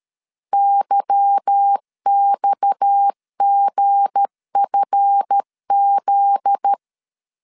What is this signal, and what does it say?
Morse code, heard as a keyed tone, about 13 wpm, 800 Hz. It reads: YXGFZ
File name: CW / DET / 1 9 1500C S A 12wpm 3800